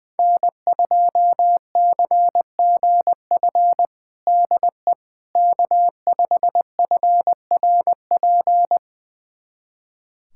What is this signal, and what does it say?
Morse code, heard as a keyed tone, about 20 wpm, 700 Hz. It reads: N2CGF DE K5FRP
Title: USA Callsign DE Callsign - 5 Characters - 20wpm